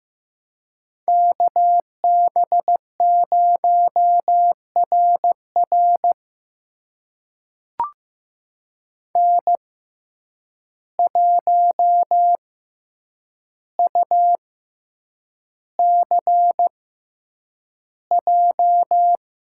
Morse code, heard as a keyed tone, about 15 wpm, spaced as for 8 wpm, 700 Hz. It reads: KB0RR N1UCJ
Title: USA Call Signs - 5 Characters - 6x Character Spacing - 15wpm